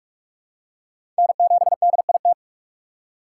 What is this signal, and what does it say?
Morse code, heard as a keyed tone, about 46 wpm, 700 Hz. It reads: N7DIT
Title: USA Call Signs - 5 Characters - 2x Character Spacing - 45wpm